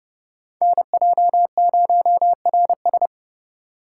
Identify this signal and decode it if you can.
Morse code, heard as a keyed tone, about 27 wpm, 700 Hz. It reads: NJ0RS